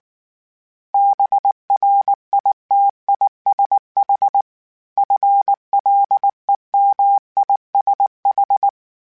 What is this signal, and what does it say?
Morse code, heard as a keyed tone, about 19 wpm, 800 Hz. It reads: BRITISH FLEMISH